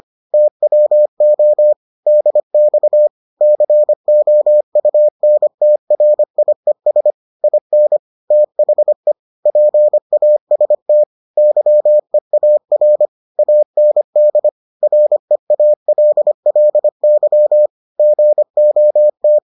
Morse code, heard as a keyed tone, about 25 wpm, 600 Hz. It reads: TWO DX COUNTRIES IN THE PAST YEAR AND REALLY GOT